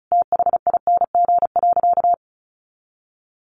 Morse code, heard as a keyed tone, about 34 wpm, 700 Hz. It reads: THING.
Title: Sentences from Top 400 Words 35wpm